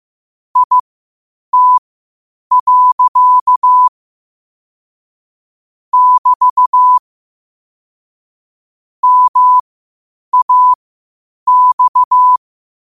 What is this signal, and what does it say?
Morse code, heard as a keyed tone, about 15 wpm, spaced as for 8 wpm, 1000 Hz. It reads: IT. <BT> MAX
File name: qotdfiles / 8wpm